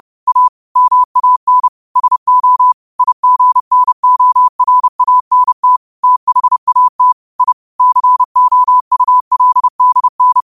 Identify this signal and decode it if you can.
Morse code, heard as a keyed tone, about 30 wpm, 1000 Hz. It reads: A MAN SO IGNORANT THAT I COULDN
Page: New quotes every day in morse code at 30 Words per minute.